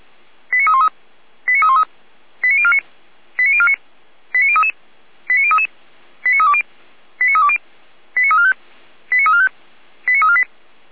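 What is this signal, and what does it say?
Five-tone selective call (ZVEI), heard as quick runs of five tones, tones of about 70 ms. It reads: 89313 89313 89049 89049 89033 89033 89320 89320 89345 89345 89348
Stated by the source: ZVEI five-tone selective call, received on an SDR.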